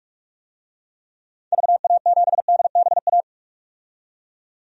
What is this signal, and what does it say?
Morse code, heard as a keyed tone, about 42 wpm, 700 Hz. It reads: VA7DBA
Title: DXCC-100 Call Signs - 6 Characters - 45wpm